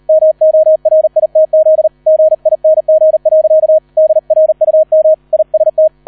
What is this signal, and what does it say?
Morse code, heard as a keyed tone, about 38 wpm, 630 Hz, with radio noise. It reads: MORITZ GING. DRUM IST